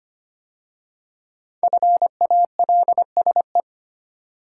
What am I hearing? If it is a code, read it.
Morse code, heard as a keyed tone, about 25 wpm, 700 Hz. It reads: FALSE
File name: 5 Letter Words - First 3 Letters Given - 25wpm